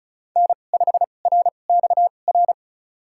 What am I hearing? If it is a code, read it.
Morse code, heard as a keyed tone, about 34 wpm, spaced as for 25 wpm, 700 Hz. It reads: N5RXR
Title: USA Call Signs - 5 Characters - 2x Character Spacing - 35wpm